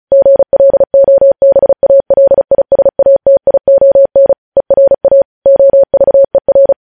Morse code, heard as a keyed tone, about 36 wpm, 550 Hz. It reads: GLOBALISATION ERA OVER
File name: podcast / News headlines-1x-35x35